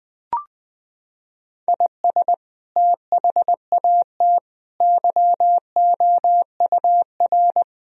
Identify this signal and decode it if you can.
Morse code, heard as a keyed tone, about 20 wpm, 700 Hz. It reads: IS THAT YOUR
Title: Sentences from Top 1000 Words - Review Part 2 - 20wpm